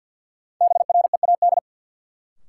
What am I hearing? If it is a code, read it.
Morse code, heard as a keyed tone, about 52 wpm, 700 Hz. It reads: BREAD
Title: Sentences from Top 1600 Words - New Word Drill - 50wpm